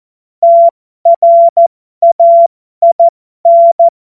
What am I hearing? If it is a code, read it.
Morse code, heard as a keyed tone, about 14 wpm, spaced as for 12 wpm, 680 Hz. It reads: TRAIN
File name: Morsekurs / Witze